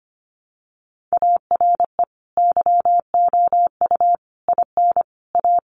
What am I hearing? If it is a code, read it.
Morse code, heard as a keyed tone, about 25 wpm, 700 Hz. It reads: ARE YOU IN A
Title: Sentences from Top 1100 Words 25wpm